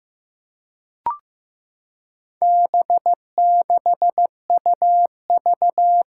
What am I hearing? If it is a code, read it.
Morse code, heard as a keyed tone, about 15 wpm, 700 Hz. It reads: B6UV
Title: Four Challenging Letter-Number Combinations 15wpm